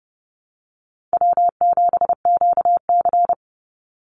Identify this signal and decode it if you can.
Morse code, heard as a keyed tone, about 30 wpm, 700 Hz. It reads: W7QC
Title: Easy Call Signs Repeated 3x - 30wpm